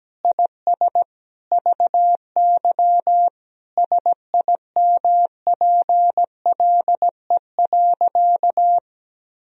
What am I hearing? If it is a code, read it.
Morse code, heard as a keyed tone, about 17 wpm, 700 Hz. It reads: IS VY SIMPLE.
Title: Sentences from Top 500 Words and Abbreviations 17wpm